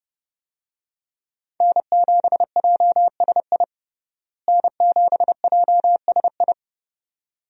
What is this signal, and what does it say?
Morse code, heard as a keyed tone, about 31 wpm, 700 Hz. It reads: N7JSI N7JSI
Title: USA Call Signs - 5 Characters - Repeated 2x - 30wpm